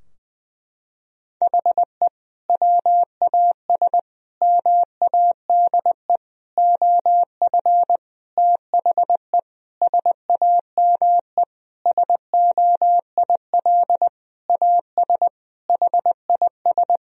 Morse code, heard as a keyed tone, about 21 wpm, 700 Hz. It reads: HE WAS MADE OF THE SAME SOIL AS HIS
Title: Sentences from Top 1000 Words - Review Part 4 - 20wpm